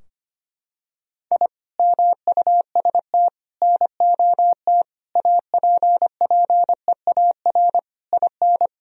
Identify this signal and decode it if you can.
Morse code, heard as a keyed tone, about 25 wpm, 700 Hz. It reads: I MUST NOT APPEAR IN
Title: Sentences from Top 1000 Words - Review Part 2 - 25wpm